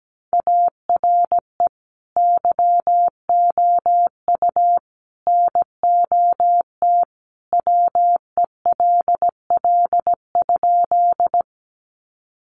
Morse code, heard as a keyed tone, about 17 wpm, 700 Hz. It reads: ARE YOU NOT WELL?
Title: Sentences from Top 600 Words 17wpm